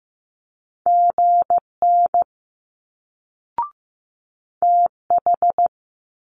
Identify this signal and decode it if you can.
Morse code, heard as a keyed tone, about 15 wpm, 700 Hz. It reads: GN TH
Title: Multi-letter Phonemes Quick Fire 15wpm